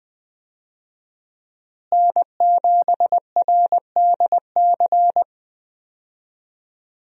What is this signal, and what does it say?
Morse code, heard as a keyed tone, about 20 wpm, 700 Hz. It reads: N7RDC
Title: USA Call Signs - 5 Characters - Repeated 2x - 20wpm